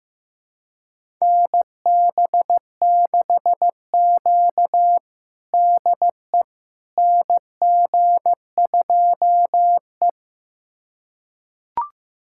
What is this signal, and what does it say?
Morse code, heard as a keyed tone, about 15 wpm, 700 Hz. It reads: NB6Q DE NG2E
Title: USA Callsign DE Callsign - 4 Characters - 15wpm